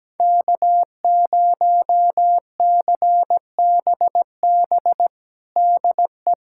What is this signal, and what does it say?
Morse code, heard as a keyed tone, about 17 wpm, 700 Hz. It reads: K0CBB DE
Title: USA Callsign DE Callsign 17wpm